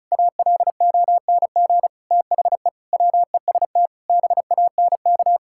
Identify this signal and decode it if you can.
Morse code, heard as a keyed tone, about 36 wpm, 700 Hz. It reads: ALONG THE WEST BANK
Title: Sentences from Top 1000 Words - Review Part 6 - 35wpm